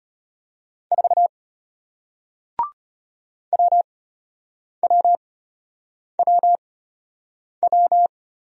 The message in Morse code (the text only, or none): 4 WWWW